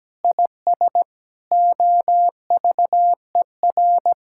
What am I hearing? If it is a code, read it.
Morse code, heard as a keyed tone, about 17 wpm, 700 Hz. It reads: IS OVER
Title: Sentences from Top 1000 Words - Review Part 5 - 17wpm